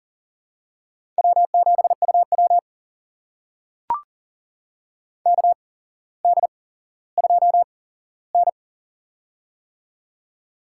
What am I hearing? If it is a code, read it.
Morse code, heard as a keyed tone, about 38 wpm, spaced as for 17 wpm, 700 Hz. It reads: W7UW KD2N
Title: USA Call Signs - 4 Characters - 8x Character Spacing - 40wpm